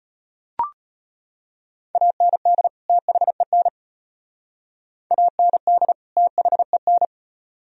Morse code, heard as a keyed tone, about 32 wpm, 700 Hz. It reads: AND THEN AND THEN